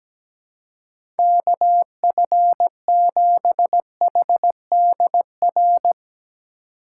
Morse code, heard as a keyed tone, about 17 wpm, 700 Hz. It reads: KF7HDR
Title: USA Call Signs - 6 Characters - 3x Character Spacing - 17wpm